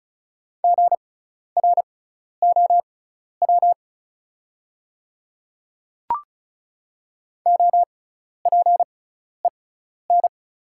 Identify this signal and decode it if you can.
Morse code, heard as a keyed tone, about 36 wpm, spaced as for 12 wpm, 690 Hz. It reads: GROW OPEN
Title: 6x Character Spacing - Top 500 Words - 35wpm